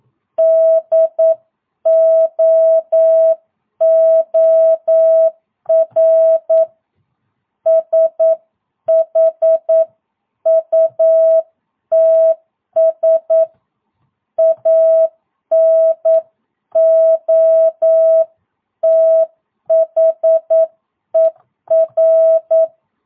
Morse code, heard as a keyed tone, about 9 wpm, 650 Hz. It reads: DOOR SHUTS ANOTHER
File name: jh8cbh / 2022cw